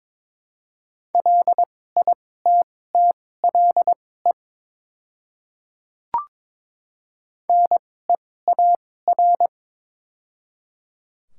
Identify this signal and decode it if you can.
Morse code, heard as a keyed tone, about 22 wpm, spaced as for 16 wpm, 700 Hz. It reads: LITTLE NEAR
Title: 2x Character Spacing - Top 200 Words - 22wpm